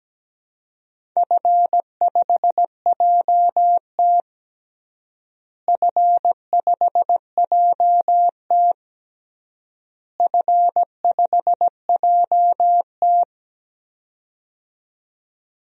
Morse code, heard as a keyed tone, about 17 wpm, 700 Hz. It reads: F5JT F5JT F5JT
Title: DXCC-100 Call Signs - 4 Characters - Repeated 3x - 17wpm